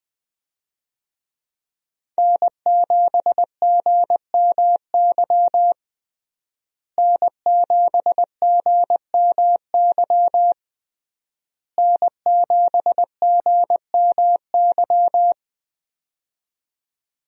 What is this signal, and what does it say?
Morse code, heard as a keyed tone, about 20 wpm, 700 Hz. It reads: N7GMY N7GMY N7GMY
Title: USA Call Signs - 5 Characters - Repeated 3x - 20wpm